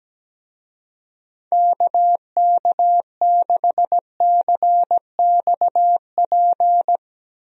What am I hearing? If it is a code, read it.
Morse code, heard as a keyed tone, about 17 wpm, 700 Hz. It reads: KK6CXP
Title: USA Call Signs - 6 Characters - Repeated 2x - 17wpm